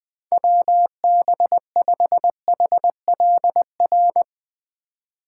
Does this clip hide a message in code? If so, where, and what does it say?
Yes: Morse code from 0.3 to 4.2 s, heard as a keyed tone, about 20 wpm, 700 Hz. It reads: WB5HLR